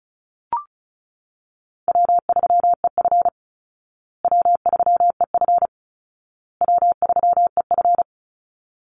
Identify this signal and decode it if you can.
Morse code, heard as a keyed tone, about 34 wpm, 700 Hz. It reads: W3EF W3EF W3EF
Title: Easy Call Signs Repeated 3x - 35wpm